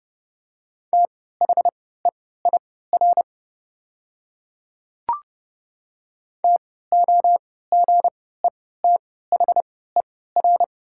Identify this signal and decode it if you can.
Morse code, heard as a keyed tone, about 29 wpm, spaced as for 17 wpm, 700 Hz. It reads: THEIR TOGETHER
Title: 3x Character Spacing - Top 500 Words - 30wpm